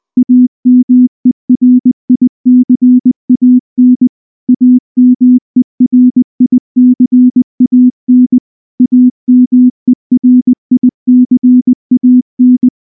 Morse code, heard as a keyed tone, about 19 wpm, 250 Hz. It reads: AMERICAN AMERICAN AMERICAN
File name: morse / morse-training-20WPM-1